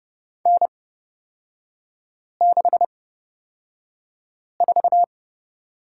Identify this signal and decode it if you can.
Morse code, heard as a keyed tone, about 30 wpm, spaced as for 5 wpm, 700 Hz. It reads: N64
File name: Lesson 36e - BT - Three Characters 30wpm